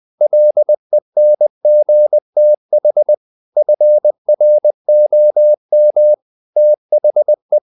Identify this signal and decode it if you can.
Morse code, heard as a keyed tone, about 20 wpm, 600 Hz. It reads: LENGTH FROM THE
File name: the old man and the sea / the old man and the sea part 1 20wpm